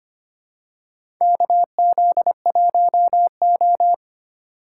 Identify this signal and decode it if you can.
Morse code, heard as a keyed tone, about 25 wpm, 700 Hz. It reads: KZ1O
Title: USA Call Signs - 4 Characters - 2x Character Spacing - 25wpm